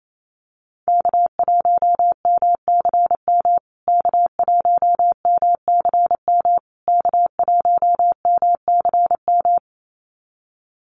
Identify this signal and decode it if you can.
Morse code, heard as a keyed tone, about 28 wpm, 700 Hz. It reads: K1MCM K1MCM K1MCM
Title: USA Call Signs Repeated 3x 28wpm